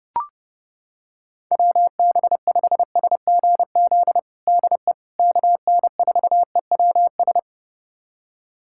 Morse code, heard as a keyed tone, about 32 wpm, 700 Hz. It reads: WB5SGZ DE KN4EWS
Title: USA Callsign DE Callsign 30wpm